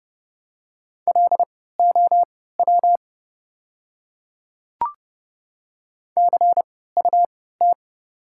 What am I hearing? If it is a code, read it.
Morse code, heard as a keyed tone, about 31 wpm, spaced as for 17 wpm, 700 Hz. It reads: LOW CUT
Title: 3x Character Spacing - Top 300 Words - 30wpm